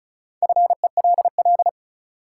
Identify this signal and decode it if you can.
Morse code, heard as a keyed tone, about 34 wpm, 700 Hz. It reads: FELL
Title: Sentences from Top 1600 Words - New Word Drill - 35wpm